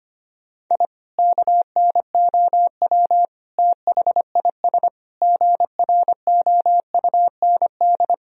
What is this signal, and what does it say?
Morse code, heard as a keyed tone, about 25 wpm, 700 Hz. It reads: I KNOW THIS GROUND
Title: Sentences from Top 1000 Words - Review Part 2 - 25wpm